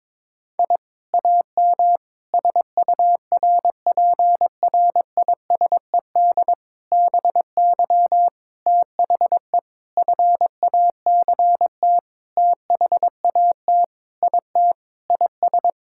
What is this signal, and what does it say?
Morse code, heard as a keyed tone, about 22 wpm, 700 Hz. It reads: I AM SURPRISED BY THE FACT THAT IT IS
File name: practice / 22wpm / Sentences with 4-Word Phrases - 22wpm